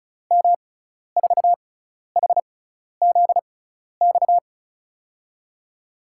Morse code, heard as a keyed tone, about 36 wpm, spaced as for 12 wpm, 700 Hz. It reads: M4HZX
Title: USA Call Signs - 6 Characters - 6x Character Spacing - 35wpm